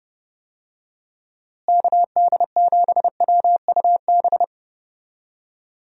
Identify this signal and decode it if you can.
Morse code, heard as a keyed tone, about 32 wpm, 700 Hz. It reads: KD7WUB
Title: USA Call Signs - 6 Characters - Repeated 3x - 30wpm